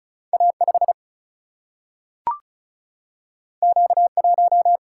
Morse code, heard as a keyed tone, about 36 wpm, 700 Hz. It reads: A5 Q1
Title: Single-Letter Single-Number 35wpm